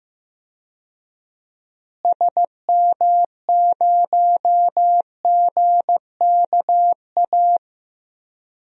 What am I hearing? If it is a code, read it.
Morse code, heard as a keyed tone, about 15 wpm, 700 Hz. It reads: SM0GKA
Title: DXCC-100 Call Signs - 6 Characters - Repeated 3x - 15wpm